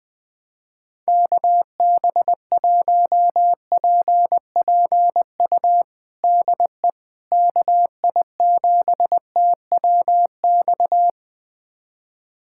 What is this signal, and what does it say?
Morse code, heard as a keyed tone, about 20 wpm, 700 Hz. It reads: KB1PPU DE KI7TWX
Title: USA Callsign DE Callsign - 6 Characters - 20wpm